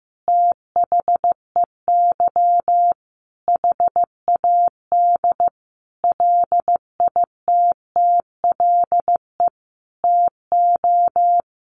Morse code, heard as a keyed tone, about 15 wpm, 700 Hz. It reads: THEY HAD LITTLE TO
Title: Sentences from Top 1300 Words 15wpm